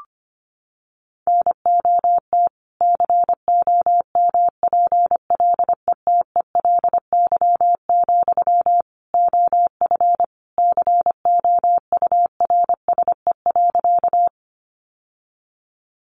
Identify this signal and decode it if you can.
Morse code, heard as a keyed tone, about 25 wpm, 700 Hz. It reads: NOT COMPLETELY, OF COURSE.
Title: Sentences from Top 1100 Words 25wpm